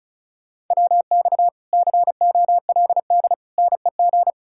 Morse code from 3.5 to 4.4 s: NEG

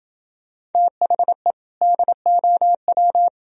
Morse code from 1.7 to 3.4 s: DOW

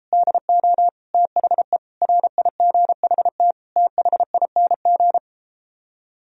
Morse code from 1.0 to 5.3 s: THE RIGHT THING

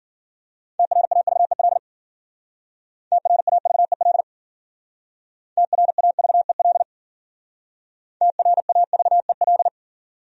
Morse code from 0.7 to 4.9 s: TRAVEL TRAVEL

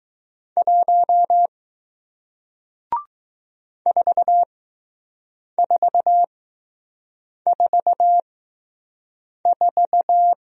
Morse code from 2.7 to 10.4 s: 4444

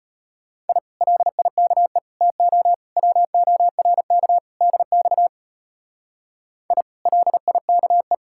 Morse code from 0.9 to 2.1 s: LIKE

